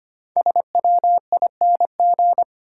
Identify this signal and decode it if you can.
Morse code, heard as a keyed tone, about 25 wpm, 700 Hz. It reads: SWING
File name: practice / 25wpm / Easy Words with Hints - 25wpm